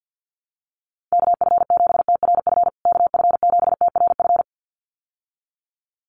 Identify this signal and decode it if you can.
Morse code, heard as a keyed tone, about 50 wpm, 700 Hz. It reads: KF7TRF KF7TRF
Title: USA Call Signs Repeated 2x 50wpm